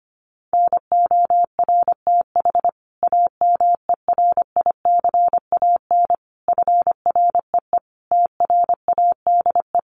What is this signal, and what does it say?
Morse code, heard as a keyed tone, about 25 wpm, 700 Hz. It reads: NORTH AMERICAN FREE TRADE
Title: Sets of 5 Words Encore 25wpm